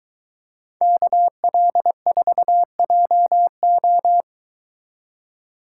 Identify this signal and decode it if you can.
Morse code, heard as a keyed tone, about 23 wpm, 700 Hz. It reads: KL4JO